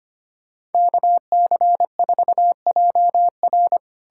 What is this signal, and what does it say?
Morse code, heard as a keyed tone, about 33 wpm, 700 Hz. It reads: KC4JR